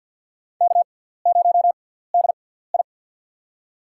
Morse code, heard as a keyed tone, about 47 wpm, spaced as for 17 wpm, 700 Hz. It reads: K0DI
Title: USA Call Signs - 4 Characters - 6x Character Spacing - 50wpm